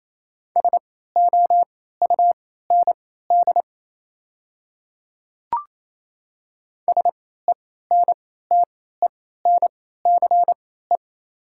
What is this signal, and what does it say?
Morse code, heard as a keyed tone, about 27 wpm, spaced as for 16 wpm, 690 Hz. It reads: SOUND SENTENCE
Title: 3x Character Spacing - Top 300 Words - 28wpm